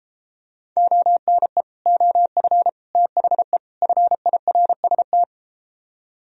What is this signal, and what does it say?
Morse code, heard as a keyed tone, about 33 wpm, 700 Hz. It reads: ONE OF THE FIRST